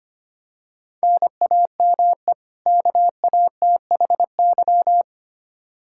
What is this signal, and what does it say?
Morse code, heard as a keyed tone, about 25 wpm, 700 Hz. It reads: NAME KATHY
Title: Practice QSO - Exchange 1 - Beginning - 25wpm